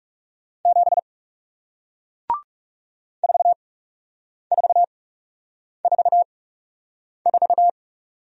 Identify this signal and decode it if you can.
Morse code, heard as a keyed tone, about 40 wpm, spaced as for 8 wpm, 700 Hz. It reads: 7 4444